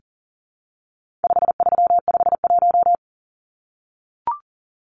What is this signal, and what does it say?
Morse code, heard as a keyed tone, about 40 wpm, 700 Hz. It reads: 5351
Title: Numbers - 4-Digit 40wpm